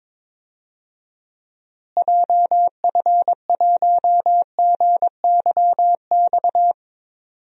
Morse code, heard as a keyed tone, about 22 wpm, 700 Hz. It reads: JF1GYX
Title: DXCC-100 Call Signs - 6 Characters - Repeated 3x - 22wpm